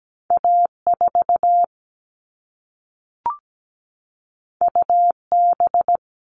Morse code, heard as a keyed tone, about 17 wpm, 700 Hz. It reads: A4 UB
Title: All Cut Numbers - 2-Digit 17wpm